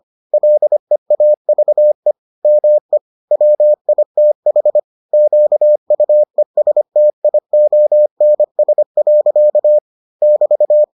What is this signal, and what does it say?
Morse code, heard as a keyed tone, about 25 wpm, 600 Hz. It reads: LEAVE ME WITH QUESTIONS. <BT>